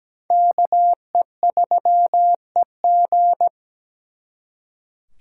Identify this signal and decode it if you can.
Morse code, heard as a keyed tone, about 20 wpm, 700 Hz. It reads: KE3EG
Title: Speed Racing - USA 2x2 Call Signs 17wpm